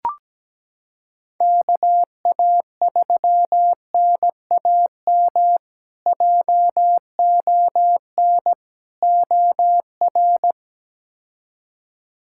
Morse code, heard as a keyed tone, about 17 wpm, 700 Hz. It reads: KA3NAM JON OR
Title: K1USN SST QSO Practice - 17wpm